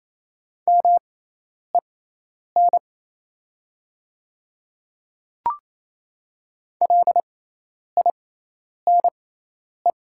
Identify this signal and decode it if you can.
Morse code, heard as a keyed tone, about 28 wpm, spaced as for 10 wpm, 690 Hz. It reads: MEN LINE
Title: 6x Character Spacing - Top 200 Words - 28wpm